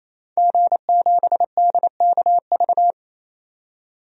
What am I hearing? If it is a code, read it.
Morse code, heard as a keyed tone, about 28 wpm, 700 Hz. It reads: G7DKV
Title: DXCC-100 Call Signs - 5 Characters - 28wpm